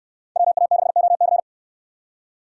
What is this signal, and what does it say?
Morse code, heard as a keyed tone, about 69 wpm, 700 Hz. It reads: WA6OC